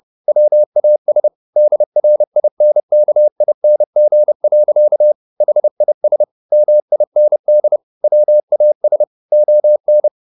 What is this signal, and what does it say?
Morse code, heard as a keyed tone, about 30 wpm, 600 Hz. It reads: WAS DRINKING. HIS MIND WAS ON
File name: the old man and the sea / the old man and the sea 1 30wpm